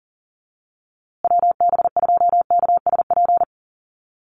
Morse code, heard as a keyed tone, about 38 wpm, 700 Hz. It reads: WB2KSP
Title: USA Call Signs Repeated 3x 40wpm